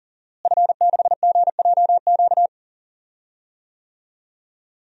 Morse code, heard as a keyed tone, about 40 wpm, 690 Hz. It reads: F6GJQ
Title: DXCC-100 Call Signs - 5 Characters - Repeated 2x - 40wpm